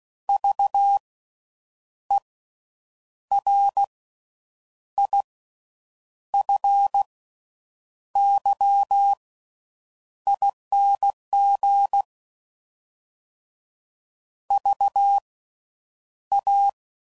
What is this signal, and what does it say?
Morse code, heard as a keyed tone, about 16 wpm, spaced as for 7 wpm, 800 Hz. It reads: VERIFYING VA
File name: Slow With Faster Ings